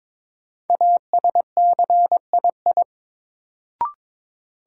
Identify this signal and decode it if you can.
Morse code, heard as a keyed tone, about 22 wpm, 700 Hz. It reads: ASCII
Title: Sets of 5 Words - New Word Drill - 22wpm